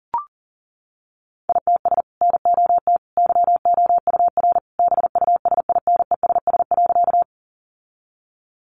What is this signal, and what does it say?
Morse code, heard as a keyed tone, about 40 wpm, 700 Hz. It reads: ITS NOT YOUR BUSINESS.
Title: Sentences from Top 600 Words 40wpm